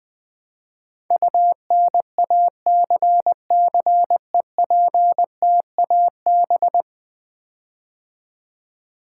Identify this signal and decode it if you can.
Morse code, heard as a keyed tone, about 20 wpm, 700 Hz. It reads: UNACCEPTAB
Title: Build A Word - 5 Letters At A Time - 12-Letter Words - 20wpm